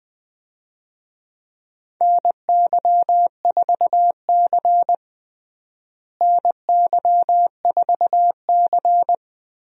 Morse code, heard as a keyed tone, about 20 wpm, 700 Hz. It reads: NY4C NY4C
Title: USA Call Signs - 4 Characters - Repeated 3x - 20wpm